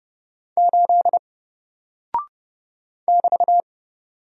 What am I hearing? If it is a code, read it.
Morse code, heard as a keyed tone, about 30 wpm, 700 Hz. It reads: 8 <BT>
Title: Single Character (Letter:Number:Punctuation:Prosign) Alt 30wpm